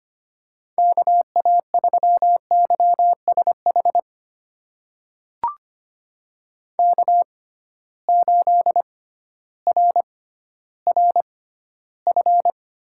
Morse code, heard as a keyed tone, about 25 wpm, spaced as for 13 wpm, 700 Hz. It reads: KA3YSH K8RRF
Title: USA Call Signs - 6x Character Spacing - 25wpm